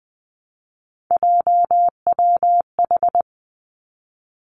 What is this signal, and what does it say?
Morse code, heard as a keyed tone, about 19 wpm, 700 Hz. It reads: JWH
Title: Three Challenging Letter-Number Combinations 20wpm